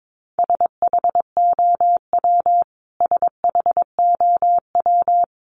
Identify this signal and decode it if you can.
Morse code, heard as a keyed tone, about 22 wpm, 700 Hz. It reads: SHOW SHOW
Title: Top 300 Words Repeated 22wpm